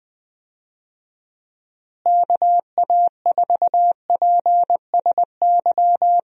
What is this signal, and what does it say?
Morse code, heard as a keyed tone, about 21 wpm, 700 Hz. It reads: KA4PSY
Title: USA Call Signs - 6 Characters - Repeated 2x - 20wpm